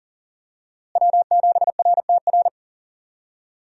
Morse code, heard as a keyed tone, about 40 wpm, 700 Hz. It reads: W7RTR
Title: USA Call Signs - 5 Characters - Repeated 3x - 40wpm